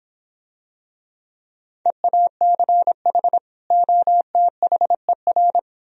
Morse code, heard as a keyed tone, about 26 wpm, 700 Hz. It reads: EACH OTHER